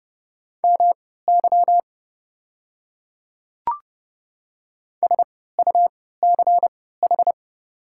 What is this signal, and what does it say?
Morse code, heard as a keyed tone, about 30 wpm, spaced as for 17 wpm, 700 Hz. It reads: MY SUCH
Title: 3x Character Spacing - Top 200 Words - 30wpm